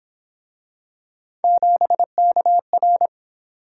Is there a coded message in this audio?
Yes: Morse code from 1.4 to 3.1 s, heard as a keyed tone, about 28 wpm, 700 Hz. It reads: 7KR